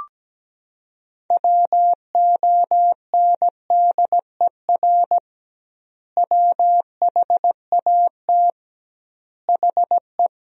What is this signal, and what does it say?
Morse code, heard as a keyed tone, about 17 wpm, 700 Hz. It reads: WONDER WHAT HE
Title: 2x Word-Spacing - Sentences from Top 500 Words - 17wpm